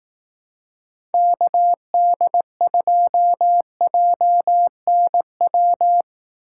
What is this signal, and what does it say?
Morse code, heard as a keyed tone, about 18 wpm, 700 Hz. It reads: KD2JNW
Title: Speed Racing - USA Call Signs 15wpm